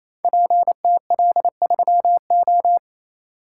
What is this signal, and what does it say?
Morse code, heard as a keyed tone, about 28 wpm, 700 Hz. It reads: PTL3O
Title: Code Groups - 5 Letter:Numbers - 28wpm